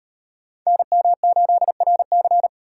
Morse code, heard as a keyed tone, about 38 wpm, 700 Hz. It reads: NM8RC